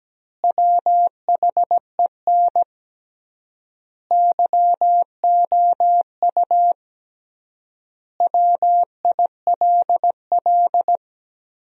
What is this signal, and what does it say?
Morse code, heard as a keyed tone, about 17 wpm, 700 Hz. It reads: WHEN YOU WILL
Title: 3x Word-Spacing - Sentences from Top 300 Words - 17wpm